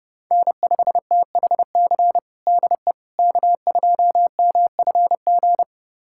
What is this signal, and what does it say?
Morse code, heard as a keyed tone, about 32 wpm, 700 Hz. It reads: N5THC DE K2MFG
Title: USA Callsign DE Callsign - 5 Characters - 30wpm